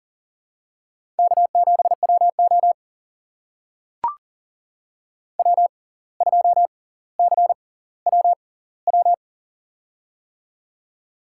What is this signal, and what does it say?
Morse code, heard as a keyed tone, about 40 wpm, spaced as for 19 wpm, 700 Hz. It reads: K7WO W2CWW
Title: USA Call Signs - 6x Character Spacing - 40wpm